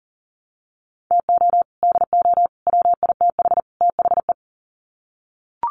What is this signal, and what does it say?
Morse code, heard as a keyed tone, about 40 wpm, 700 Hz. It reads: TO DO WITH THE
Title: Sets of 4 Words 40wpm